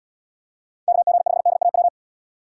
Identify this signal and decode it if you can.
Morse code, heard as a keyed tone, about 76 wpm, 700 Hz. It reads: KG5DAW